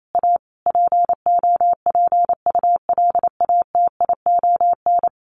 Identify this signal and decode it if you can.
Morse code, heard as a keyed tone, about 28 wpm, 700 Hz. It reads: A POPULATION
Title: A Sound Of Thunder 28wpm